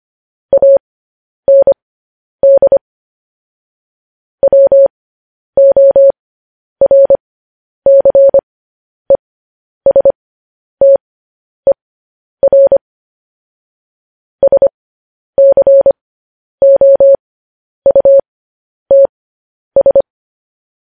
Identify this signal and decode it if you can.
Morse code, heard as a keyed tone, about 25 wpm, spaced as for 10 wpm, 550 Hz. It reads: AND WORCESTER SCOUTS
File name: podcast / News headlines-1x-25x10